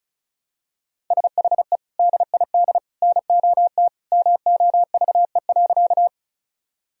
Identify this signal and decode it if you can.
Morse code, heard as a keyed tone, about 36 wpm, 700 Hz. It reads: SHE DID NOT MOVE.
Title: Sentences from Top 1000 Words - Review Part 1 - 35wpm